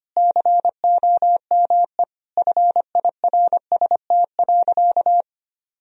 Morse code, heard as a keyed tone, about 25 wpm, 700 Hz. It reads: COME FIRST.
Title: Sentences from Top 1000 Words - Review Part 4 - 25wpm